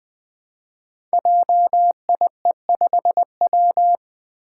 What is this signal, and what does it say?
Morse code, heard as a keyed tone, about 20 wpm, 700 Hz. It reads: JIE5W
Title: Code Groups - 5 Letter:Numbers - 20wpm